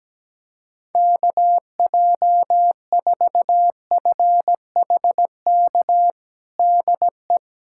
Morse code, heard as a keyed tone, about 17 wpm, 700 Hz. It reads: KJ4FHK DE
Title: USA Callsign DE Callsign 17wpm